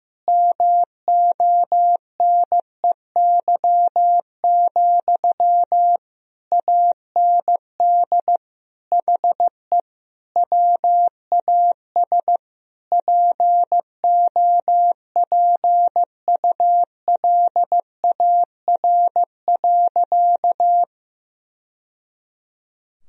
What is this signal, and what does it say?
Morse code, heard as a keyed tone, about 15 wpm, 700 Hz. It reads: MONEY, AND HE WAS POPULAR.
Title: Sentences from Top 1000 Words - Review Part 3 - 15wpm